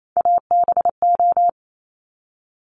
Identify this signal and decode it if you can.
Morse code, heard as a keyed tone, about 28 wpm, 700 Hz. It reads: ABO
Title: Top Three Letter Combinations in Words 28wpm